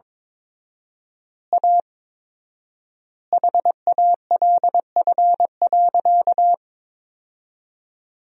Morse code, heard as a keyed tone, about 21 wpm, 700 Hz. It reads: A HALF.
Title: 4x Word-Spacing - Sentences from Top 400 Words - 22wpm